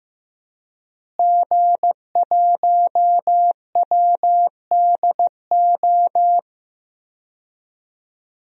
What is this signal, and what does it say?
Morse code, heard as a keyed tone, about 15 wpm, 700 Hz. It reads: G1WDO
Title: DXCC-100 Call Signs - 5 Characters - Repeated 2x - 15wpm